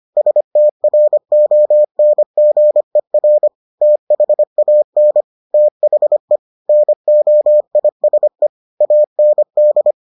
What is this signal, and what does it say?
Morse code, heard as a keyed tone, about 25 wpm, 600 Hz. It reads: STRONGER THAN THE NOISE AND